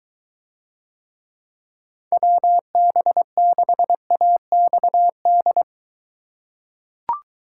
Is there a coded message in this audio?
Yes: Morse code from 2.1 to 5.6 s, heard as a keyed tone, about 20 wpm, 700 Hz. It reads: WB6AXD